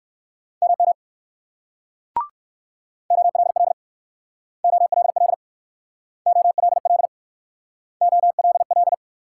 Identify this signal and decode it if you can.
Morse code, heard as a keyed tone, about 57 wpm, 690 Hz. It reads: MM OLL OLL OLL OLL